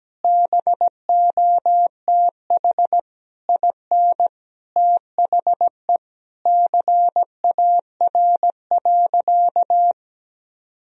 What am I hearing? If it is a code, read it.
Morse code, heard as a keyed tone, about 17 wpm, 700 Hz. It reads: BOTH IN THE CAR.
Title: Sentences from Top 1000 Words - Review Part 1 - 17wpm